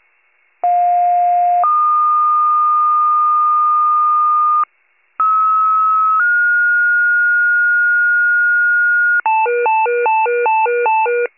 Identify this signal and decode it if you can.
Two-tone page, heard as a one-second tone, then a three-second tone, about 710 then 1220 Hz, about 1290 then 1430 Hz, with noise. It is heard from 0.6 to 9.2 s.
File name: pager / 84 FIRE / wcalert